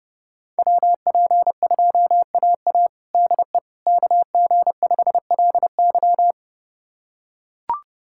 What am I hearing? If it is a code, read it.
Morse code, heard as a keyed tone, about 30 wpm, 700 Hz. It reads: WP2AA DE KG5LY
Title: USA Callsign DE Callsign - 5 Characters - 30wpm